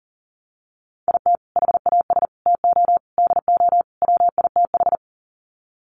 Morse code, heard as a keyed tone, about 40 wpm, 700 Hz. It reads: IT HAS TO DO WITH
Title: Sets of 5 Words 40wpm